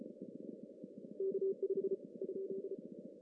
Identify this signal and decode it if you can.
Morse code, heard as a keyed tone, about 34 wpm, 390 Hz. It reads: K5 ?
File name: CW / cqww-cw-k5tr